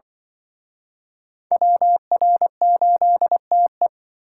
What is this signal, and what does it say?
Morse code, heard as a keyed tone, about 25 wpm, 700 Hz. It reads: WR8TE
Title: Speed Racing - USA 2x2 Call Signs 20wpm